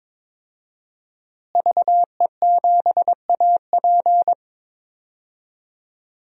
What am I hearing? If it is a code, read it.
Morse code, heard as a keyed tone, about 22 wpm, 700 Hz. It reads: VE7AP
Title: DXCC-100 Call Signs - 5 Characters - Repeated 3x - 22wpm